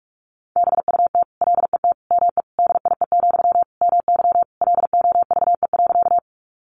Morse code, heard as a keyed tone, about 44 wpm, 700 Hz. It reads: BUT LET ME DIE, MY LOVE.
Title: Sentences from Top 1300 Words 45wpm